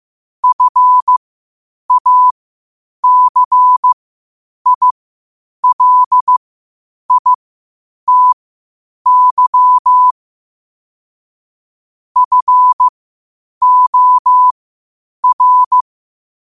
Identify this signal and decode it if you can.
Morse code, heard as a keyed tone, about 15 wpm, spaced as for 8 wpm, 1000 Hz. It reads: FACILITY FOR
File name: qotdfiles / 8wpm